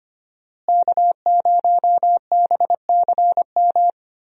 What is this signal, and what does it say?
Morse code, heard as a keyed tone, about 25 wpm, 700 Hz. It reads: K0BCM
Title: USA Call Signs - 5 Characters - Repeated 2x - 25wpm